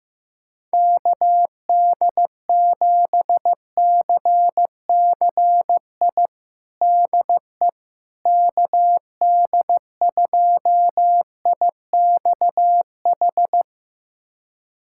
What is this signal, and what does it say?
Morse code, heard as a keyed tone, about 15 wpm, 700 Hz. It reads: KD7CCI DE KD2IXH
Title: USA Callsign DE Callsign - 6 Characters - 15wpm